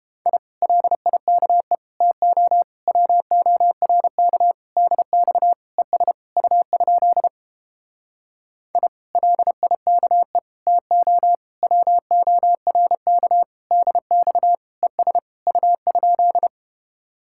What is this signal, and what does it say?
Morse code, heard as a keyed tone, about 31 wpm, 700 Hz. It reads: I LIKE TO WORK DX ES U? I LIKE TO WORK DX ES U?